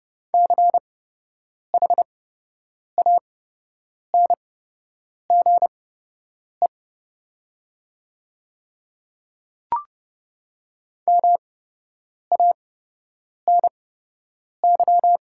Morse code, heard as a keyed tone, about 30 wpm, spaced as for 8 wpm, 700 Hz. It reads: CHANGE MANY
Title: 8x Character Spacing - Top 200 Words - 30wpm